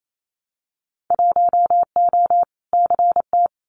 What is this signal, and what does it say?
Morse code, heard as a keyed tone, about 28 wpm, 700 Hz. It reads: 1O CT
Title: Winter Field Day Exchanges 28wpm